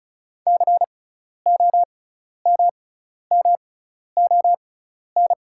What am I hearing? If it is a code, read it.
Morse code, heard as a keyed tone, about 36 wpm, spaced as for 12 wpm, 700 Hz. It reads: COMMON
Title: 6x Character Spacing - Top 500 Words - 35wpm